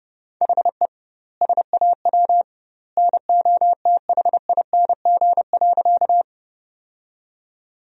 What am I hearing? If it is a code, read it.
Morse code, heard as a keyed tone, about 30 wpm, 700 Hz. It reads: HE SAW NOTHING.
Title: 2x Word-Spacing - Sentences from Top 500 Words - 30wpm